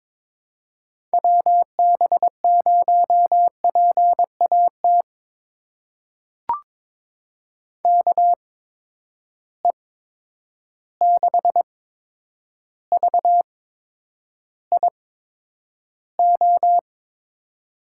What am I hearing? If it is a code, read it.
Morse code, heard as a keyed tone, about 22 wpm, spaced as for 9 wpm, 700 Hz. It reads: WB0PAT KE6VIO
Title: USA Call Signs - 8x Character Spacing - 22wpm